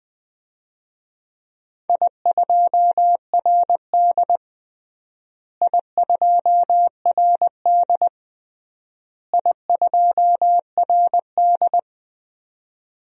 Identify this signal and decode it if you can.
Morse code, heard as a keyed tone, about 21 wpm, 690 Hz. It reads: I2RD I2RD I2RD
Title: DXCC-100 Call Signs - 4 Characters - Repeated 3x - 20wpm